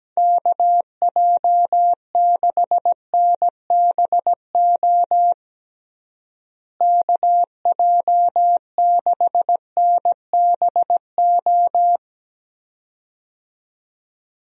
Morse code, heard as a keyed tone, about 17 wpm, 700 Hz. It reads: KJ6NBO KJ6NBO
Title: USA Call Signs - 6 Characters - Repeated 2x - 17wpm